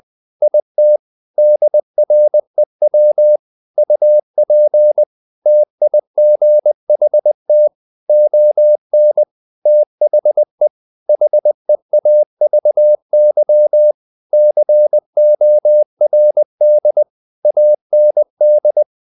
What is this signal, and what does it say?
Morse code, heard as a keyed tone, about 20 wpm, 600 Hz. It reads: IT DREW UP TIGHT ON THE HEAVY CORD AND
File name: the old man and the sea / the old man and the sea part 1 20wpm